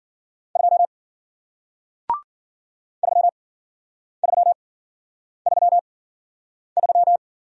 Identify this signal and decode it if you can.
Morse code, heard as a keyed tone, about 54 wpm, spaced as for 9 wpm, 700 Hz. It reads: 2 3333